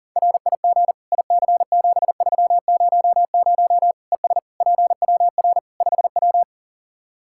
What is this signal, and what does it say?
Morse code, heard as a keyed tone, about 40 wpm, 700 Hz. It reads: RIG IC7300 ES PWR 5W
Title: Practice QSO - Exchange 2 - Info - 40wpm